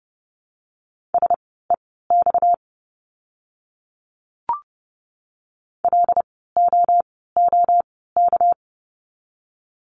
Morse code, heard as a keyed tone, about 30 wpm, spaced as for 17 wpm, 700 Hz. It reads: SEX LOOK
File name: course / 30:10wpm / 230 Lesson 37f - Letter x - Words 30wpm